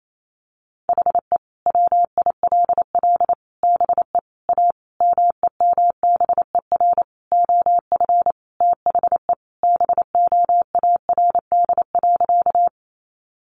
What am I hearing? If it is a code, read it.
Morse code, heard as a keyed tone, about 28 wpm, 700 Hz. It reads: HE WILL BE A MEMBER OF THE BOARD.
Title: Sentences from Top 1100 Words 28wpm